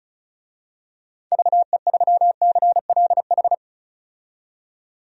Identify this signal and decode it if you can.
Morse code, heard as a keyed tone, about 34 wpm, 700 Hz. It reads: VE3CLH
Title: DXCC-100 Call Signs - 6 Characters - 35wpm